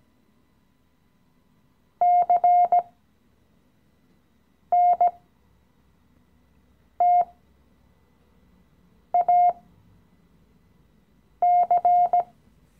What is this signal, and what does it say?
Morse code, heard as a keyed tone, about 17 wpm, spaced as for 4 wpm, 710 Hz. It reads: CNTAC
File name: CW-Online-Kurs / Anfänger / Lektion 015